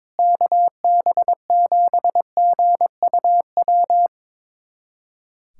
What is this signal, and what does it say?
Morse code, heard as a keyed tone, about 22 wpm, 700 Hz. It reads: KB7GUW
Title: USA Callsign DE Callsign - 6 Characters - 22wpm